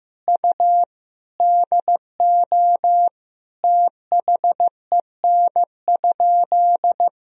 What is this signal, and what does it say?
Morse code, heard as a keyed tone, about 15 wpm, 700 Hz. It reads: U DO THEN?
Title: Sentences from Top 100 Words and Abbreviations 15wpm